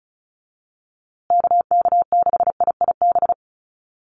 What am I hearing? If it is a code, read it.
Morse code, heard as a keyed tone, about 34 wpm, 700 Hz. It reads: KK6IIB
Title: USA Call Signs Repeated 2x 35wpm